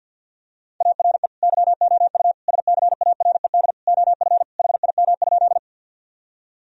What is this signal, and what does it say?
Morse code, heard as a keyed tone, about 50 wpm, 700 Hz. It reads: ARE YOU SCARED OF HIM?
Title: Sentences from Top 1000 Words - Review Part 4 - 50wpm